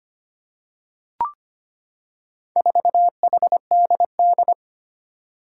Morse code, heard as a keyed tone, about 25 wpm, 700 Hz. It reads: 4HDD
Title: Four Challenging Letter-Number Combinations 25wpm